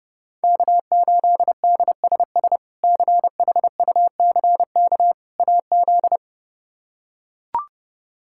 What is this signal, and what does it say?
Morse code, heard as a keyed tone, about 32 wpm, 700 Hz. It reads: K8DSS CHUCK AZ
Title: K1USN SST QSO Practice - 30wpm